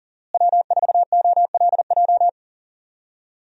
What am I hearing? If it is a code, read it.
Morse code, heard as a keyed tone, about 40 wpm, 700 Hz. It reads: W4OLJ
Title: USA Call Signs - 5 Characters - Repeated 2x - 40wpm